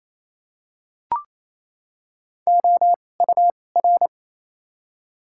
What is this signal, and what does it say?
Morse code, heard as a keyed tone, about 28 wpm, 700 Hz. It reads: O U R
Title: 2x Character Spacing - Top 200 Words - 28wpm